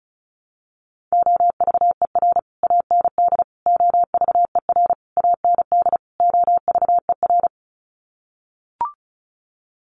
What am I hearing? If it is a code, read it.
Morse code, heard as a keyed tone, about 34 wpm, 700 Hz. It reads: OVER AND OVER AND OVER
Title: Sets of 5 Words Encore 35wpm